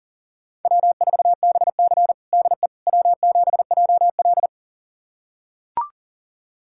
Morse code, heard as a keyed tone, about 37 wpm, 700 Hz. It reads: W4BC DE W7JL
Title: USA Callsign DE Callsign - 4 Characters - 40wpm